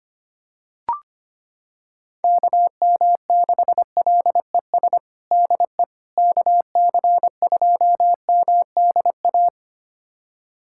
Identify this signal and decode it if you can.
Morse code, heard as a keyed tone, about 25 wpm, 700 Hz. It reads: KM6LES DE KC2MDA
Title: USA Callsign DE Callsign - 6 Characters - 25wpm